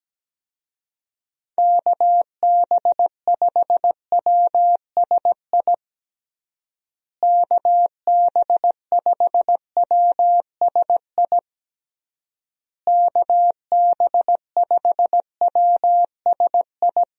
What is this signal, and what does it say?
Morse code, heard as a keyed tone, about 17 wpm, 700 Hz. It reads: KB5WSI KB5WSI KB5WSI
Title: USA Call Signs - 6 Characters - Repeated 3x - 17wpm